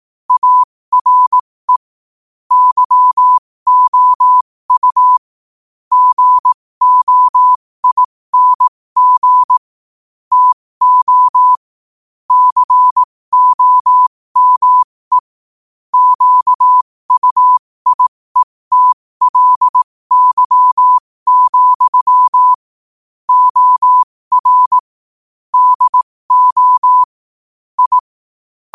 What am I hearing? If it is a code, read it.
Morse code, heard as a keyed tone, about 18 wpm, spaced as for 15 wpm, 1000 Hz. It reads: ARE YOU GOING TO COME QUIETLY, OR DO I
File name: qotdfiles / 15wpm